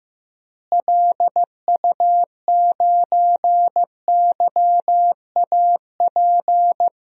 Morse code, heard as a keyed tone, about 15 wpm, 700 Hz. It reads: LU9YAP
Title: DXCC-100 Call Signs - 6 Characters - Repeated 3x - 15wpm